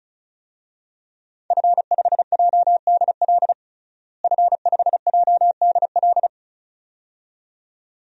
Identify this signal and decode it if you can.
Morse code, heard as a keyed tone, about 36 wpm, 700 Hz. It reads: F5JDL F5JDL
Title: DXCC-100 Call Signs - 5 Characters - Repeated 2x - 35wpm